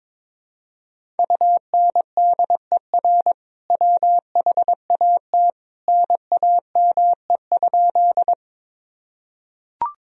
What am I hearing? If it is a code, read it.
Morse code, heard as a keyed tone, about 22 wpm, 700 Hz. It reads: UNDER WHAT NAME?
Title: Sentences from Top 1000 Words - Review Part 1 - 22wpm